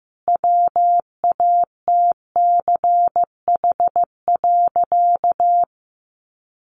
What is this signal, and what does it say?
Morse code, heard as a keyed tone, about 15 wpm, 700 Hz. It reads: WATCH.
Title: A Sound Of Thunder 15wpm